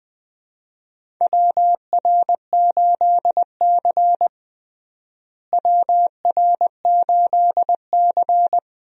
Morse code, heard as a keyed tone, about 20 wpm, 700 Hz. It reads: WR8C WR8C
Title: USA Call Signs - 4 Characters - Repeated 2x - 20wpm